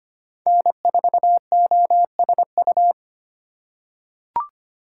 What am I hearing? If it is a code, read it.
Morse code, heard as a keyed tone, about 25 wpm, 700 Hz. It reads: N4OSU
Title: USA Call Signs - 5 Characters - 4x Character Spacing - 25wpm